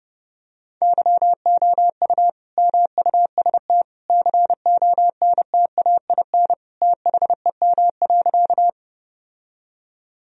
Morse code, heard as a keyed tone, about 30 wpm, 700 Hz. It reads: YOU MUST CONTAIN THEM.
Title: Sentences from Top 1000 Words - Review Part 2 - 30wpm